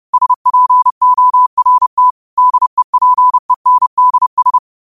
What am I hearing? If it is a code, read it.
Morse code, heard as a keyed tone, about 30 wpm, 1000 Hz. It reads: SPORT DEPENDS